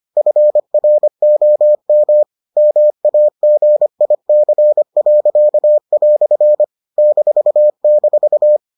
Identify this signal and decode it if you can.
Morse code, heard as a keyed tone, about 25 wpm, 600 Hz. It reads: FROM MAGIC." --